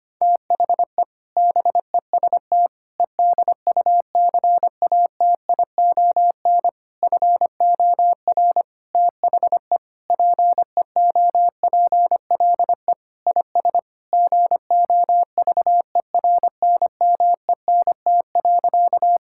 Morse code, heard as a keyed tone, about 25 wpm, 700 Hz. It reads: THE BEST EDUCATION FOR THE PEOPLE IS GOVERNMENT.
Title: Sentences from Top 1000 Words - Review Part 6 - 25wpm